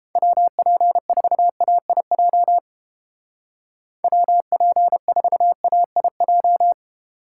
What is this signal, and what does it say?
Morse code, heard as a keyed tone, about 29 wpm, 700 Hz. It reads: WP4AIJ WP4AIJ